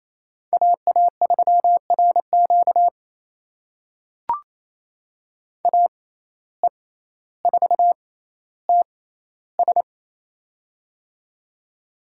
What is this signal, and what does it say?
Morse code, heard as a keyed tone, about 28 wpm, spaced as for 14 wpm, 700 Hz. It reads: AA3RQ AE4TS
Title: USA Call Signs - 5 Characters - 6x Character Spacing - 28wpm